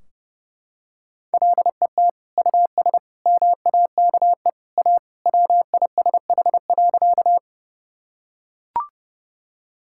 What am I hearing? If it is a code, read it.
Morse code, heard as a keyed tone, about 30 wpm, 700 Hz. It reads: LET US MAKE A WISH.
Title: Sentences from Top 1600 Words 30wpm